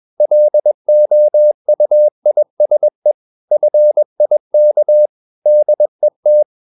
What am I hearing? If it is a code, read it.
Morse code, heard as a keyed tone, about 21 wpm, 600 Hz. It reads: LOUISE FIK DET
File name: Ferie QRS 21wpm